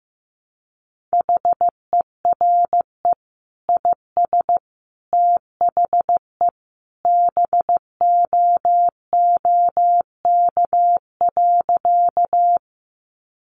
Morse code, heard as a keyed tone, about 15 wpm, 700 Hz. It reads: HERE IS THE BOOK.
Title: Sentences from Top 300 Words 15wpm